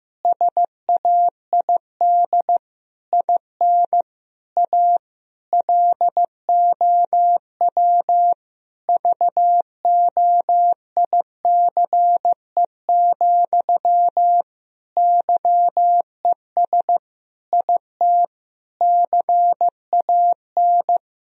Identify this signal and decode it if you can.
Morse code, heard as a keyed tone, about 15 wpm, 700 Hz. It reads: SAID IN A LOW VOICE, YES IT CAN
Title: Sentences from Top 1000 Words - Review Part 1 - 15wpm